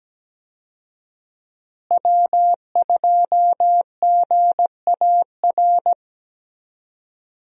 Morse code, heard as a keyed tone, about 17 wpm, 700 Hz. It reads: W2GAR
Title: USA Call Signs - 5 Characters - Repeated 2x - 17wpm